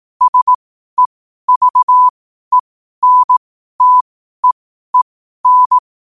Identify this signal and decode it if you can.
Morse code, heard as a keyed tone, about 18 wpm, spaced as for 12 wpm, 1000 Hz. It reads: SEVENTEEN
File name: qotdfiles / 12wpm